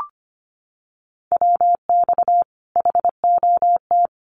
Morse code, heard as a keyed tone, about 25 wpm, 700 Hz. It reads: WX HOT
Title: Common QSO Phrases 25wpm